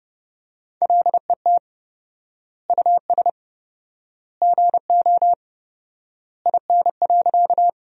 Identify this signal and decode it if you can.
Morse code, heard as a keyed tone, about 30 wpm, 700 Hz. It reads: LET US GO IN.
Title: 4x Word-Spacing - Sentences from Top 300 Words - 30wpm